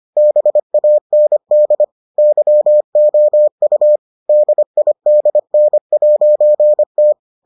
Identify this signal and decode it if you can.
Morse code, heard as a keyed tone, about 25 wpm, 600 Hz. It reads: BAND YOU DIDN'T